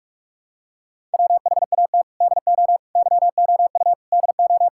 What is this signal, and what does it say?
Morse code, heard as a keyed tone, about 46 wpm, 700 Hz. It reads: WHAT DO YOU DO
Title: Sentences from Top 1000 Words - Review Part 6 - 45wpm